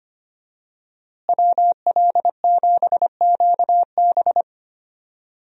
Morse code, heard as a keyed tone, about 25 wpm, 700 Hz. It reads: WL7QB
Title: USA Call Signs - 5 Characters - Repeated 2x - 25wpm